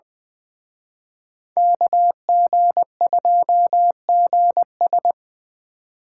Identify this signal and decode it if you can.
Morse code, heard as a keyed tone, about 23 wpm, 700 Hz. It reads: KG2GS
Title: Speed Racing - USA 2x2 Call Signs 20wpm